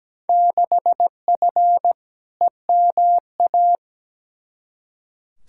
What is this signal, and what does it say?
Morse code, heard as a keyed tone, about 17 wpm, 700 Hz. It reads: 6F EMA
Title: Field Day Exchanges Encore - 17wpm